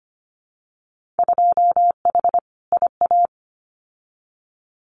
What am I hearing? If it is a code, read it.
Morse code, heard as a keyed tone, about 25 wpm, 700 Hz. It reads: 2H IA
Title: Winter Field Day Exchanges Encore 25wpm